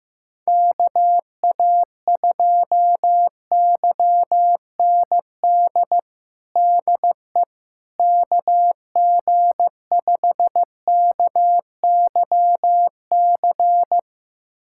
Morse code, heard as a keyed tone, about 16 wpm, 700 Hz. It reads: KA2YND DE KG5KYC
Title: USA Callsign DE Callsign - 6 Characters - 15wpm